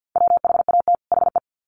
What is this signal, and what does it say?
Morse code, heard as a keyed tone, about 50 wpm, 700 Hz. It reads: WHAT HE
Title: Sets of 2 Words 50wpm